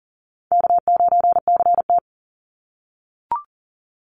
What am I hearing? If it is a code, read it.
Morse code, heard as a keyed tone, about 40 wpm, 700 Hz. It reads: K9CT
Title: Call Signs - 40wpm